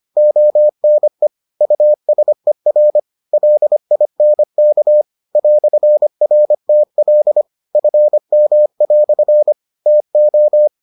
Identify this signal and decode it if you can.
Morse code, heard as a keyed tone, about 25 wpm, 600 Hz. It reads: ONE USER LINK "RTL FM" TO